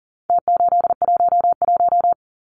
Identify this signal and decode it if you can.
Morse code, heard as a keyed tone, about 40 wpm, 700 Hz. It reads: T811
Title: Numbers - Contest Style - 4-Digit 40wpm